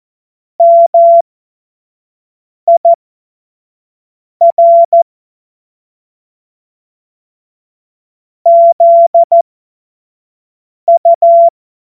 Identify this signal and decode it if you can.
Morse code, heard as a keyed tone, about 14 wpm, spaced as for 5 wpm, 680 Hz. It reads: MIR ZU
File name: Morsekurs / Witze / Witz 31.Oktober 2023 25 F70